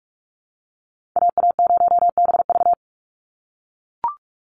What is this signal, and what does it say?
Morse code, heard as a keyed tone, about 44 wpm, 700 Hz. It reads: AA0BV
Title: NA SOTA Activator Callsigns 45wpm